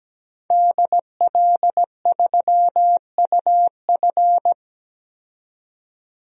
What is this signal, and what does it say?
Morse code, heard as a keyed tone, about 17 wpm, 700 Hz. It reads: DL3UF
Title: DXCC-100 Call Signs - 5 Characters - 17wpm